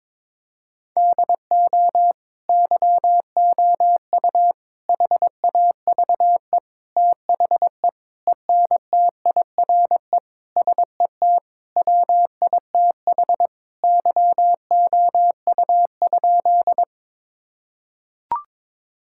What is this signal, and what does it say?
Morse code, heard as a keyed tone, about 22 wpm, 700 Hz. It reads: DO YOU HAVE THE ENTIRE SET WITH YOU?
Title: Sentences from Top 1000 Words - Review Part 5 - 22wpm